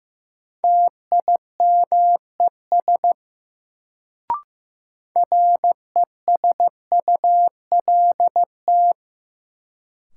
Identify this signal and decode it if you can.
Morse code, heard as a keyed tone, about 15 wpm, 700 Hz. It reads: TIMES RESULT
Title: Sets of 3 Words - New Word Drill - 15wpm